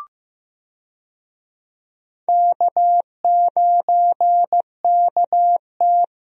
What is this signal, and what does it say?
Morse code, heard as a keyed tone, about 15 wpm, 700 Hz. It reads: K9KT
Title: USA Call Signs - 4 Characters - Repeated 2x - 15wpm